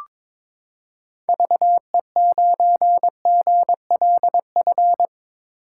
Morse code, heard as a keyed tone, about 21 wpm, 700 Hz. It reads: VE9GLF
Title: K1USN SST QSO Practice - 22wpm